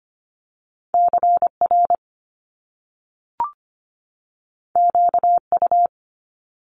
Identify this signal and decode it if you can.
Morse code, heard as a keyed tone, about 25 wpm, 700 Hz. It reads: CR QU
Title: Top Two Letter Combinations in Words 25wpm